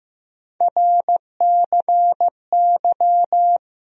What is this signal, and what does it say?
Morse code, heard as a keyed tone, about 15 wpm, 700 Hz. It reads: RCY
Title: USA Call Sign Postfixes - 3 Letters 15wpm